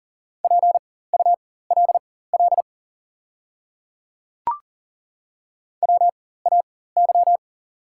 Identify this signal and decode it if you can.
Morse code, heard as a keyed tone, about 40 wpm, spaced as for 19 wpm, 690 Hz. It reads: PULL WAY
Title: 4x Character Spacing - Top 500 Words - 40wpm